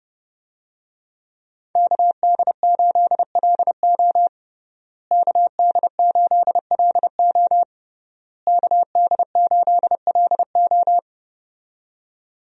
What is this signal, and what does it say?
Morse code, heard as a keyed tone, about 32 wpm, 700 Hz. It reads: KD8LO KD8LO KD8LO
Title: USA Call Signs - 5 Characters - Repeated 3x - 30wpm